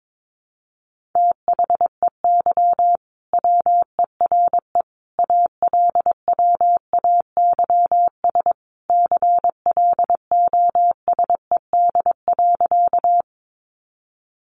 Morse code, heard as a keyed tone, about 22 wpm, 700 Hz. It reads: THEY WERE ALWAYS CLOSED.
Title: Sentences from Top 1400 Words 22wpm